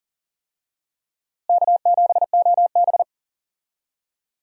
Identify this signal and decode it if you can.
Morse code, heard as a keyed tone, about 37 wpm, 700 Hz. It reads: K7OB
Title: USA Call Signs - 4 Characters - Repeated 2x - 40wpm